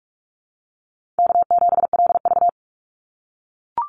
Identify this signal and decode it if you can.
Morse code, heard as a keyed tone, about 44 wpm, 700 Hz. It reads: K7LV
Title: Call Signs - 45wpm